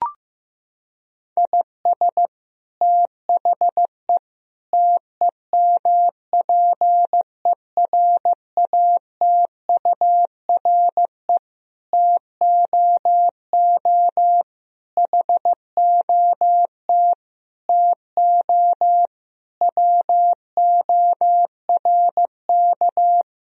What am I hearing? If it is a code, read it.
Morse code, heard as a keyed tone, about 16 wpm, 700 Hz. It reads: IS THE TEMPERATURE TOO HOT TO WORK
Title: Sentences from Top 1000 Words - Review Part 4 - 15wpm